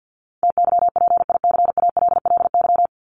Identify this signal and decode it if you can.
Morse code, heard as a keyed tone, about 50 wpm, 700 Hz. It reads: TYPICALLY
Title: LY Suffix - Top 100 Words with Base Words 50wpm